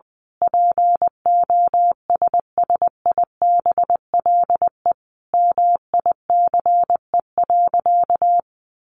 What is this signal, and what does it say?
Morse code, heard as a keyed tone, about 20 wpm, 700 Hz. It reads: POSSIBLE MICE.
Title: A Sound Of Thunder 20wpm